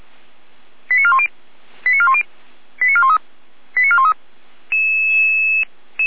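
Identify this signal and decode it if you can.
Five-tone selective call (ZVEI), heard as quick runs of five tones, tones of about 70 ms. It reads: 89410 89410 89413 89413
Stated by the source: ZVEI five-tone selective call received on an SDR.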